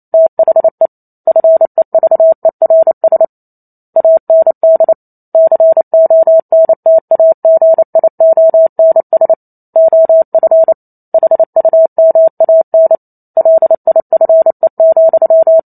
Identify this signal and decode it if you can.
Morse code, heard as a keyed tone, about 28 wpm, 650 Hz. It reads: THE FEVERS AND CONTAGIONS OF HUMAN LIFE,